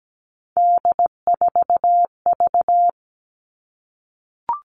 Morse code, heard as a keyed tone, about 17 wpm, 700 Hz. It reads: D4V
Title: All Cut Numbers - 3-Digit 17wpm